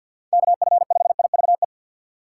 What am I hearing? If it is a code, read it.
Morse code, heard as a keyed tone, about 48 wpm, 700 Hz. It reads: KF5IVE
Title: USA Call Signs - 6 Characters - 4x Character Spacing - 50wpm